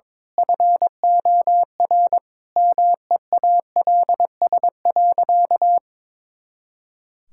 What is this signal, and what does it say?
Morse code, heard as a keyed tone, about 22 wpm, 700 Hz. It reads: FOR MEALS.
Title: Pangram Challenge v2 - 22wpm